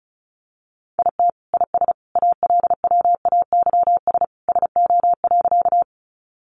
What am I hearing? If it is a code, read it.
Morse code, heard as a keyed tone, about 34 wpm, 700 Hz. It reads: IT IS ALWAYS SO.
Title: Sentences from Top 300 Words 35wpm